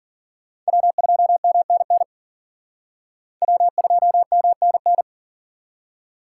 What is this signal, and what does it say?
Morse code, heard as a keyed tone, about 51 wpm, 700 Hz. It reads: W2MNN W2MNN